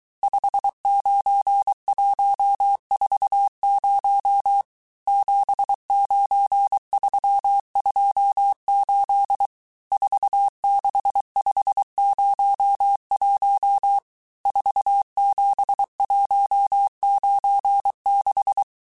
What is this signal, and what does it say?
Morse code, heard as a keyed tone, about 23 wpm, 800 Hz. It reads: 59140 79328 46501 47196
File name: cours-du-mercredi / audios / cours-855-texte-chiffre-vitesse-22-mpm